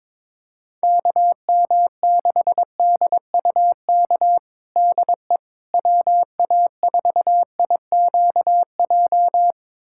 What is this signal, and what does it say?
Morse code, heard as a keyed tone, about 22 wpm, 700 Hz. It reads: KM6DUK DE WA4IQJ
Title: USA Callsign DE Callsign - 6 Characters - 22wpm